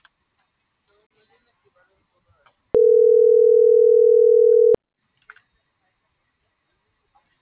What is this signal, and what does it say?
Ringback tone (North America), 440 + 480 Hz.